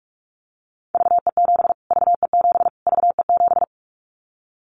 Morse code, heard as a keyed tone, about 44 wpm, 700 Hz. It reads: VE7 VE7 VE7
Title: Callsign Prefixes for Top 25 DXCC - Repeated 3x - 45wpm